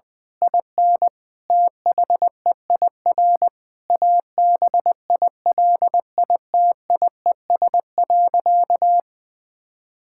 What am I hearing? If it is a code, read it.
Morse code, heard as a keyed tone, about 20 wpm, 700 Hz. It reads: IN THEIR ABILITIES.
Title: Difficult Sentences with Hints - 20wpm